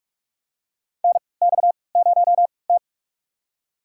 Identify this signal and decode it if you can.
Morse code, heard as a keyed tone, about 42 wpm, spaced as for 25 wpm, 690 Hz. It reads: NX0T
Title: USA Call Signs - 4 Characters - 3x Character Spacing - 45wpm